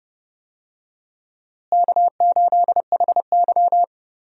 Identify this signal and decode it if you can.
Morse code, heard as a keyed tone, about 31 wpm, 700 Hz. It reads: K8HY
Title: USA Call Signs - 4 Characters - Repeated 2x - 30wpm